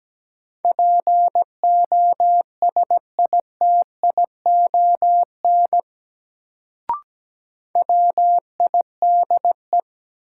Morse code, heard as a keyed tone, about 17 wpm, 700 Hz. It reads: POSITION WIDE
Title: Sentences from Top 900 Words - New Word Drill - 17wpm